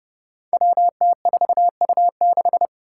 Morse code, heard as a keyed tone, about 29 wpm, 700 Hz. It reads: WT4U6
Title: Five Challenging Letter-Number Combinations 30wpm